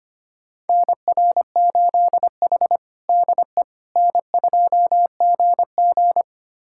Morse code, heard as a keyed tone, about 25 wpm, 700 Hz. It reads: NR8H DE N2GG
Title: USA Callsign DE Callsign - 4 Characters - 25wpm